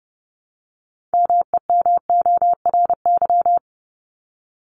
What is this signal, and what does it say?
Morse code, heard as a keyed tone, about 30 wpm, 700 Hz. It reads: MEMORY
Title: Top 1251-1500 Words 30wpm